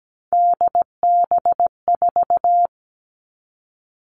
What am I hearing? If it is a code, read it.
Morse code, heard as a keyed tone, about 17 wpm, 700 Hz. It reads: DB4
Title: All Cut Numbers - 3-Digit 17wpm